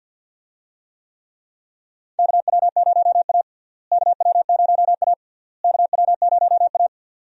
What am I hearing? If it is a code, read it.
Morse code, heard as a keyed tone, about 46 wpm, 700 Hz. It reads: KW0A KW0A KW0A
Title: USA Call Signs - 4 Characters - Repeated 3x - 50wpm